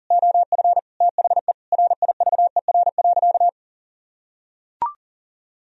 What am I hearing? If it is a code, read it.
Morse code, heard as a keyed tone, about 40 wpm, 700 Hz. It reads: OF THE RIVER.
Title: Sentences from Top 1000 Words - Review Part 1 - 40wpm